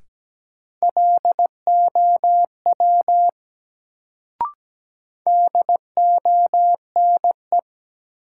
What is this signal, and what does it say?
Morse code, heard as a keyed tone, about 17 wpm, 700 Hz. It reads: LOW DONE
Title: Sentences from Top 500 Words - New Word Drill - 17wpm